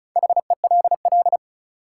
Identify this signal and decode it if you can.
Morse code, heard as a keyed tone, about 34 wpm, 700 Hz. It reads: HELL
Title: Sets of 5 Words Encore - New Word Drill - 35wpm